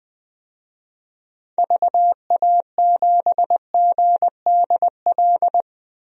Morse code, heard as a keyed tone, about 21 wpm, 700 Hz. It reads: VA7GDL
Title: DXCC-100 Call Signs - 6 Characters - 20wpm